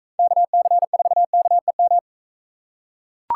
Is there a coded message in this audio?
Yes: Morse code from 0.2 to 2.0 s, heard as a keyed tone, about 43 wpm, 700 Hz. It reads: KC4KEM